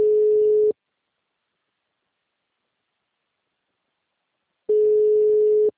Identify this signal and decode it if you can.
Ringback tone (Europe, 425 Hz).
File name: out- -2018- - - .45634